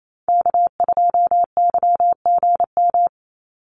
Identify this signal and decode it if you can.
Morse code, heard as a keyed tone, about 28 wpm, 700 Hz. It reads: K2YGM
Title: Call Signs Repeated 2x - 28wpm